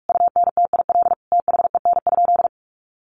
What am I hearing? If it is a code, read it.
Morse code, heard as a keyed tone, about 44 wpm, 700 Hz. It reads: UNTIL THEN?
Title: Sentences from Top 500 Words 45wpm